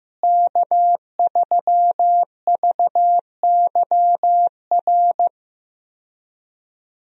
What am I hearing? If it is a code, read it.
Morse code, heard as a keyed tone, about 15 wpm, 700 Hz. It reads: K3VYR
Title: USA Call Signs - 5 Characters - Repeated 3x - 15wpm